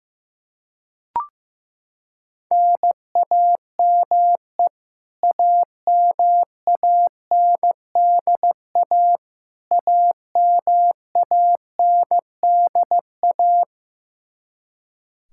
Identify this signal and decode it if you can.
Morse code, heard as a keyed tone, about 15 wpm, 700 Hz. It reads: NAME AMANDA AMANDA
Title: K1USN SST QSO Practice 15wpm